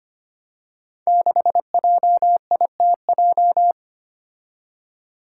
Morse code, heard as a keyed tone, about 25 wpm, 700 Hz. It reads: 6JITJ
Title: Five Challenging Letter-Number Combinations 25wpm